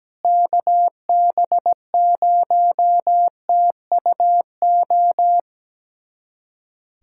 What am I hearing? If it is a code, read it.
Morse code, heard as a keyed tone, about 17 wpm, 700 Hz. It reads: KB0TUO
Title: USA Callsign DE Callsign 17wpm